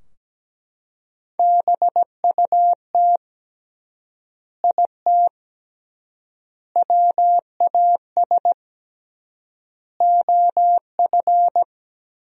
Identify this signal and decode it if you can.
Morse code, heard as a keyed tone, about 17 wpm, 700 Hz. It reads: BUT IT WAS OF
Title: 3x Word-Spacing - Sentences from Top 100 Words - 17wpm